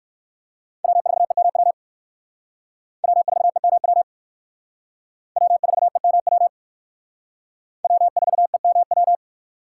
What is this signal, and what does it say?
Morse code, heard as a keyed tone, about 56 wpm, 700 Hz. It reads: W4EMW W4EMW W4EMW W4EMW
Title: Speed Racing - USA 1x3 Call Signs 45wpm